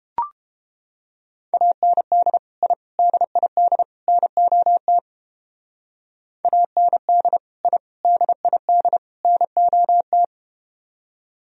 Morse code, heard as a keyed tone, about 31 wpm, 700 Hz. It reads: AND I DID NOT AND I DID NOT